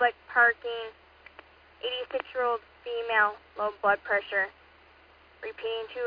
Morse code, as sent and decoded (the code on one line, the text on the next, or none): none